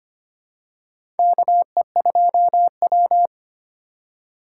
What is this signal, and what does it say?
Morse code, heard as a keyed tone, about 25 wpm, 700 Hz. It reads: KE2W
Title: USA Call Signs - 4 Characters - 4x Character Spacing - 25wpm